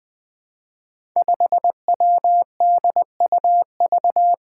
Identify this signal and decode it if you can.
Morse code, heard as a keyed tone, about 20 wpm, 700 Hz. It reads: 5WDUV
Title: Five Challenging Letter-Number Combinations 20wpm